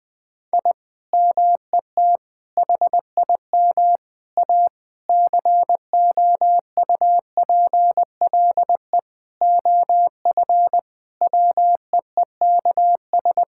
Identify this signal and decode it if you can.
Morse code, heard as a keyed tone, about 20 wpm, 700 Hz. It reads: I MET HIM A COUPLE OF WEEKS
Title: Sentences with 5-Word Phrases - Spoken Beginning - 20wpm